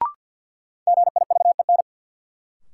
Morse code, heard as a keyed tone, about 52 wpm, 700 Hz. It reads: GIVEN
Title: Sentences from Top 1400 Words - New Word Drill Alt - 50wpm